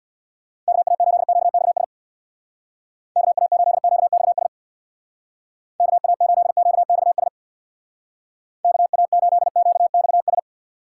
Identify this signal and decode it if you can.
Morse code, heard as a keyed tone, about 64 wpm, 700 Hz. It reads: KA8QXS KA8QXS KA8QXS KA8QXS